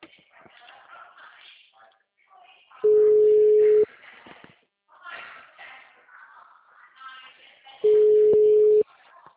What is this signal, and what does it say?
Ringback tone (Europe, 425 Hz), with speech.